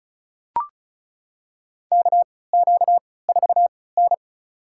Morse code, heard as a keyed tone, about 36 wpm, spaced as for 20 wpm, 700 Hz. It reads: KQ4N
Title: USA Call Signs - 4 Characters - 3x Character Spacing - 35wpm